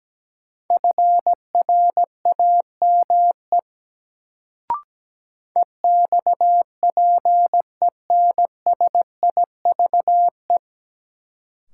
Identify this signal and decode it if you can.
Morse code, heard as a keyed tone, about 17 wpm, 700 Hz. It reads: FRAME EXPENSIVE
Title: Sentences from Top 1000 Words - Review Part 6 - New Word Drill - 17wpm